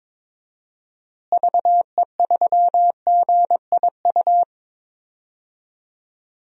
Morse code, heard as a keyed tone, about 22 wpm, 700 Hz. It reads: VE3GIU
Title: DXCC-100 Call Signs - 6 Characters - Repeated 3x - 22wpm